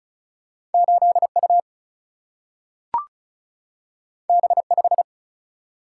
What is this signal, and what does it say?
Morse code, heard as a keyed tone, about 36 wpm, 700 Hz. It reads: 8U B5
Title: Single-Letter Single-Number 35wpm